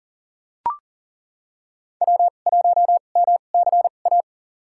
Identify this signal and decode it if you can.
Morse code, heard as a keyed tone, about 40 wpm, spaced as for 29 wpm, 700 Hz. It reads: W1MCA
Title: USA Call Signs - 5 Characters - 2x Character Spacing - 40wpm